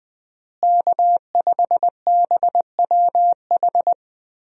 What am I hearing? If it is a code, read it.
Morse code, heard as a keyed tone, about 21 wpm, 700 Hz. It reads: K5BWH
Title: USA Call Signs - 5 Characters - Repeated 2x - 20wpm